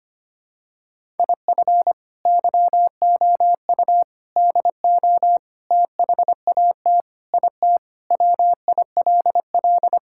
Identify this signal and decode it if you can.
Morse code, heard as a keyed tone, about 25 wpm, 700 Hz. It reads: IF YOU DO THAT IT WILL
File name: Sentences with 3-Word Phrases - 25wpm